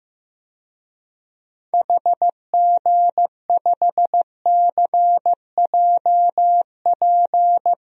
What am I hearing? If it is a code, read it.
Morse code, heard as a keyed tone, about 15 wpm, 700 Hz. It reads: HG5CJP
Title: DXCC-100 Call Signs - 6 Characters - 15wpm